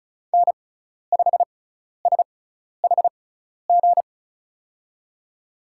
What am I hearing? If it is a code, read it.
Morse code, heard as a keyed tone, about 36 wpm, spaced as for 12 wpm, 700 Hz. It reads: N5SHG
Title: USA Call Signs - 5 Characters - 6x Character Spacing - 35wpm